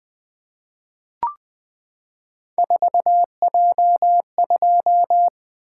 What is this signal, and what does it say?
Morse code, heard as a keyed tone, about 20 wpm, 700 Hz. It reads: 4J2
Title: Three Letter-Number Combinations - 20wpm